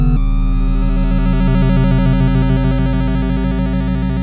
There is no code.